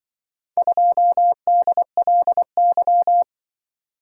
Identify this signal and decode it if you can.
Morse code, heard as a keyed tone, about 24 wpm, 700 Hz. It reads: 2DLY
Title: Speed-Racing - Four Letter-Number Combinations - 20wpm